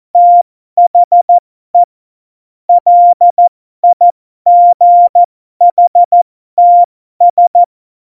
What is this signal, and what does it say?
Morse code, heard as a keyed tone, about 14 wpm, spaced as for 12 wpm, 700 Hz. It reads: THE LIGHTS
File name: Morsekurs / Witze